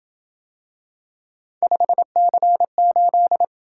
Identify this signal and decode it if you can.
Morse code, heard as a keyed tone, about 26 wpm, 700 Hz. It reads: 5C8